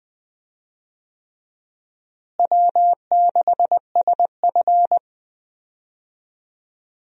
Morse code, heard as a keyed tone, about 20 wpm, 700 Hz. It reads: W6SF
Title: USA Call Signs - 4 Characters - Repeated 3x - 20wpm